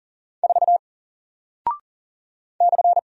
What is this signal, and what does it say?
Morse code, heard as a keyed tone, about 40 wpm, 700 Hz. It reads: 4 /
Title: Single Character (Letter:Number:Punctuation:Prosign) Alt 40wpm